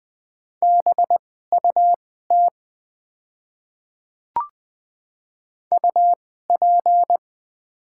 Morse code, heard as a keyed tone, about 20 wpm, spaced as for 15 wpm, 700 Hz. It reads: BUT UP
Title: 2x Character Spacing - Top 100 Words - 20wpm